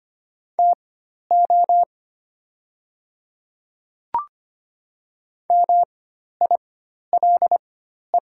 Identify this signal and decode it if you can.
Morse code, heard as a keyed tone, about 25 wpm, spaced as for 12 wpm, 700 Hz. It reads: TO MILE
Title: 4x Character Spacing - Top 300 Words - 25wpm